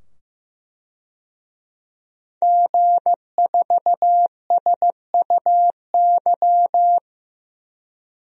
Morse code, heard as a keyed tone, about 15 wpm, 700 Hz. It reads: G4SUY
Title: DXCC-100 Call Signs - 5 Characters - 15wpm